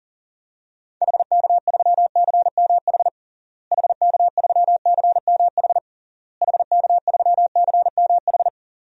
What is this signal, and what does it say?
Morse code, heard as a keyed tone, about 40 wpm, 700 Hz. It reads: HK3CMH HK3CMH HK3CMH
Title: DXCC-100 Call Signs - 6 Characters - Repeated 3x - 40wpm